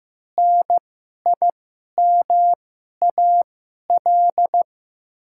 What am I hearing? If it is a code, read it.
Morse code, heard as a keyed tone, about 15 wpm, spaced as for 11 wpm, 700 Hz. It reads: NIMAL
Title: 2x Character Spacing - Top 500 Words - 15wpm